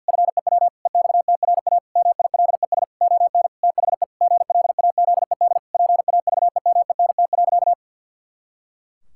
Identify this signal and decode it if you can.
Morse code, heard as a keyed tone, about 50 wpm, 700 Hz. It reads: FEW EXTRA MILES ON THE GLAZED PAVEMENT.
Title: Pangram Challenge v2 - 50wpm